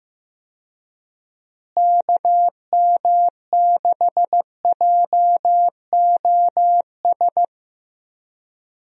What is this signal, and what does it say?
Morse code, heard as a keyed tone, about 15 wpm, 700 Hz. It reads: KM6JOS
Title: USA Call Signs - 6 Characters - Repeated 3x - 15wpm